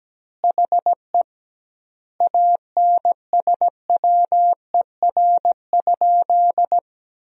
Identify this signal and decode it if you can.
Morse code, heard as a keyed tone, about 17 wpm, 700 Hz. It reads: HE ANSWER?
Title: 2x Word-Spacing - Sentences from Top 300 Words - 17wpm